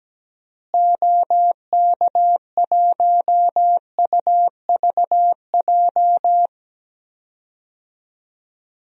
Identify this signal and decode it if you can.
Morse code, heard as a keyed tone, about 17 wpm, 700 Hz. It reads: OK1UVJ
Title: DXCC-100 Call Signs - 6 Characters - Repeated 2x - 17wpm